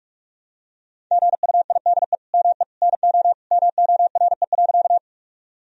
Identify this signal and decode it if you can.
Morse code, heard as a keyed tone, about 44 wpm, 700 Hz. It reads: GUIDE ME NO MORE.
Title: Sentences from Top 1000 Words - Review Part 3 - 45wpm